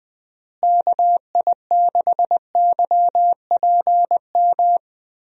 Morse code, heard as a keyed tone, about 20 wpm, 700 Hz. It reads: KI6YPM
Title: USA Call Signs - 6 Characters - Repeated 3x - 20wpm